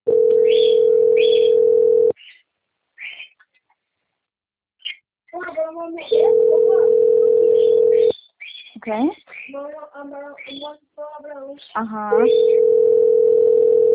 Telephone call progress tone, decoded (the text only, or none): ringback tone